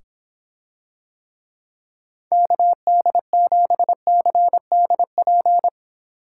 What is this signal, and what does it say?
Morse code, heard as a keyed tone, about 26 wpm, 700 Hz. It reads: KD7CDP